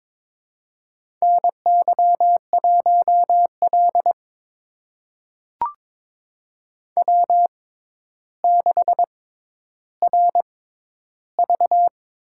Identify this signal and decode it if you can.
Morse code, heard as a keyed tone, about 22 wpm, spaced as for 11 wpm, 700 Hz. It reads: NY1L W6RV
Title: USA Call Signs - 4 Characters - 6x Character Spacing - 22wpm